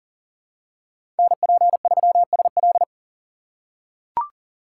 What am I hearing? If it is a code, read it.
Morse code, heard as a keyed tone, about 40 wpm, 700 Hz. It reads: NP3SL
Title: USA Call Signs - 5 Characters - 4x Character Spacing - 40wpm